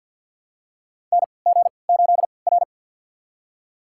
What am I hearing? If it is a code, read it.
Morse code, heard as a keyed tone, about 53 wpm, spaced as for 29 wpm, 700 Hz. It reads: NG8R
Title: USA Call Signs - 4 Characters - 3x Character Spacing - 50wpm